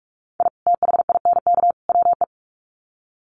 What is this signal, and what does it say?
Morse code, heard as a keyed tone, about 44 wpm, 700 Hz. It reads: I THINK WE
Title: Sets of 3 Words 45wpm